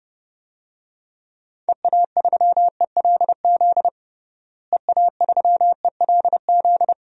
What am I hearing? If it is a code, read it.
Morse code, heard as a keyed tone, about 30 wpm, 700 Hz. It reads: EA3ELZ EA3ELZ
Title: DXCC-100 Call Signs - 6 Characters - Repeated 2x - 30wpm